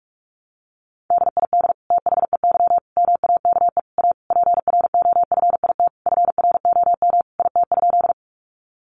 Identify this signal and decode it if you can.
Morse code, heard as a keyed tone, about 44 wpm, 700 Hz. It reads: DID THEY MAKE A PROFIT FROM IT?
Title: Sentences from Top 800 Words 45wpm